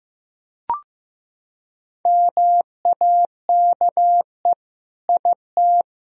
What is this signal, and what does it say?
Morse code, heard as a keyed tone, about 15 wpm, 700 Hz. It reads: MAKE IT
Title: Sentences from Top 1000 Words - Review Part 3 - 15wpm